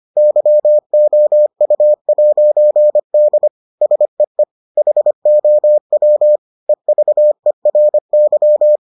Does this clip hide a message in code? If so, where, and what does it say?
Yes: Morse code from 0.2 to 8.8 s, heard as a keyed tone, about 25 wpm, 600 Hz. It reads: YOU'D SEE HOW EVERY